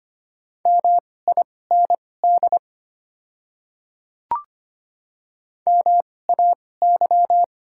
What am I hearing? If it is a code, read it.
Morse code, heard as a keyed tone, about 25 wpm, spaced as for 18 wpm, 690 Hz. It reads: MIND MAY
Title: 2x Character Spacing - Top 500 Words - 25wpm